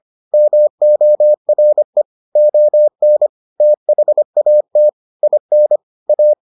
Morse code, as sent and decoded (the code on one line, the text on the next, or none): -- --- .-. / --- -. / - .... .- - / .. -. / .-
MORE ON THAT IN A